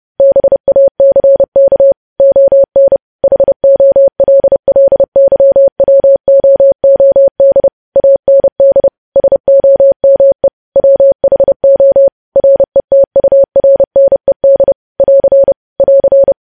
Morse code, heard as a keyed tone, about 30 wpm, 550 Hz. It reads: BACK ON HOLLYWOOD AND SOME WHO RETURNED <AR> <AR>
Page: news headlines in morse code at 30 wpm